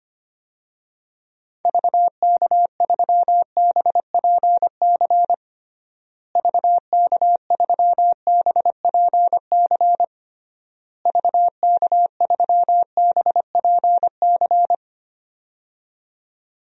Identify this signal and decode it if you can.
Morse code, heard as a keyed tone, about 25 wpm, 700 Hz. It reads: VK3BPC VK3BPC VK3BPC
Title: DXCC-100 Call Signs - 6 Characters - Repeated 3x - 25wpm